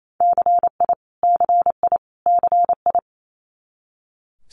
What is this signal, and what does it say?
Morse code, heard as a keyed tone, about 28 wpm, 700 Hz. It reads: CI CI CI
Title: Multi-letter Phonemes Repeated 3x 28wpm